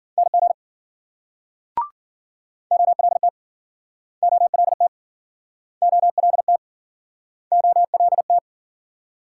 Morse code, heard as a keyed tone, about 55 wpm, 700 Hz. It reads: NG OLT OLT OLT OLT